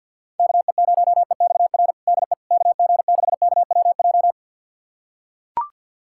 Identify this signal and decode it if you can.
Morse code, heard as a keyed tone, about 45 wpm, 700 Hz. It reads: KE0EXR DE KG6KWJ
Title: USA Callsign DE Callsign - 6 Characters - 50wpm